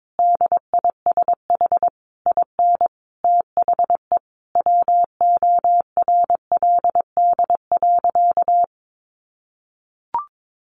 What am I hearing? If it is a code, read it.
Morse code, heard as a keyed tone, about 22 wpm, 700 Hz. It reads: DISH IN THE WORLD.
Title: Sentences from Top 1300 Words 22wpm